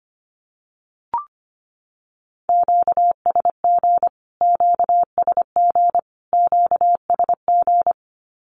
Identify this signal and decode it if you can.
Morse code, heard as a keyed tone, about 25 wpm, 700 Hz. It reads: QSG QSG QSG
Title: Q-Codes Repeated 3x 25wpm